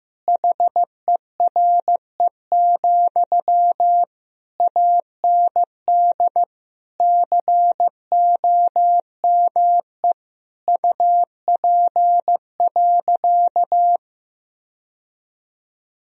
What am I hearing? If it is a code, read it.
Morse code, heard as a keyed tone, about 15 wpm, 700 Hz. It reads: HERE, AND COME UP.
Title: Sentences from Top 1000 Words - Review Part 2 - 15wpm